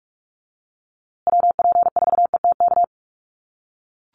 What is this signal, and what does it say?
Morse code, heard as a keyed tone, about 44 wpm, 700 Hz. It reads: WP4ETK
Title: USA Call Signs Repeated 2x 45wpm